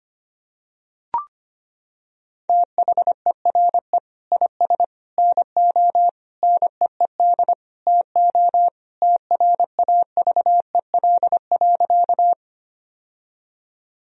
Morse code, heard as a keyed tone, about 25 wpm, 700 Hz. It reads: THERE IS NO NEED TO TRAVEL.
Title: Sentences from Top 1000 Words - Review Part 2 - 25wpm